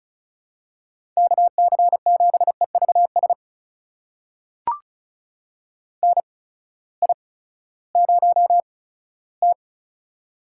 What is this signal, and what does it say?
Morse code, heard as a keyed tone, about 36 wpm, spaced as for 18 wpm, 700 Hz. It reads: KC7EVS NI0T
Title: USA Call Signs - 8x Character Spacing - 35wpm